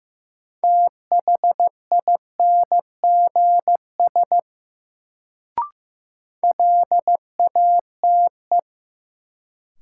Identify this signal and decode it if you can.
Morse code, heard as a keyed tone, about 15 wpm, 700 Hz. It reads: THINGS LATE
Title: Sets of 3 Words - New Word Drill - 15wpm